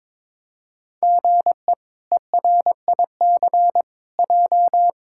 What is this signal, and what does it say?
Morse code, heard as a keyed tone, about 22 wpm, 700 Hz. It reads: GE ERIC J
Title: K1USN SST QSO Practice - 22wpm